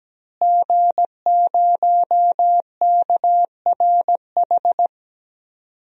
Morse code, heard as a keyed tone, about 17 wpm, 700 Hz. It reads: G0KRH
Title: DXCC-100 Call Signs - 5 Characters - Repeated 2x - 17wpm